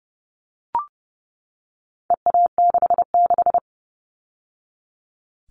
Morse code, heard as a keyed tone, about 30 wpm, 700 Hz. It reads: EA66
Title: All Cut Numbers - 4-Digit 30wpm